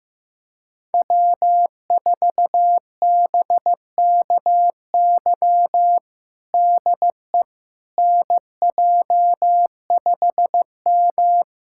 Morse code, heard as a keyed tone, about 15 wpm, 700 Hz. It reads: W4BKY DE NJ5M
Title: USA Callsign DE Callsign 15wpm